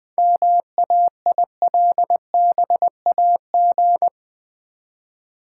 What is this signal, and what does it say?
Morse code, heard as a keyed tone, about 20 wpm, 700 Hz. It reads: MAILBAG
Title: Easy Words with Hints - 20wpm